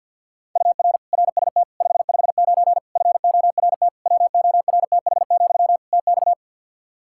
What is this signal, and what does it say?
Morse code, heard as a keyed tone, about 48 wpm, 700 Hz. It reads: UR RST 559 FORT WORTH, TX
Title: Practice QSO - Exchange 1 - Beginning - 50wpm